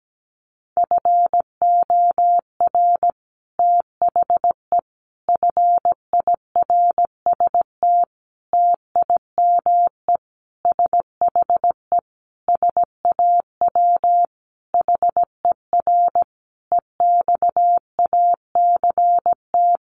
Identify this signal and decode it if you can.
Morse code, heard as a keyed tone, about 17 wpm, 700 Hz. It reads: FOR THE FIRST TIME SHE SAW HER EXACT
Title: Sentences from Top 1300 Words 17wpm